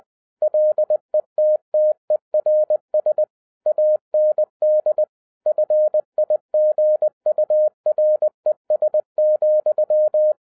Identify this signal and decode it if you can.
Morse code, heard as a keyed tone, about 20 wpm, 610 Hz. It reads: LETTERS AND FIGURES,